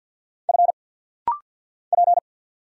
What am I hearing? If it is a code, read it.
Morse code, heard as a keyed tone, about 50 wpm, 690 Hz. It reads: F P
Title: Single Letter-Number Alt - Rapid-Fire 50wpm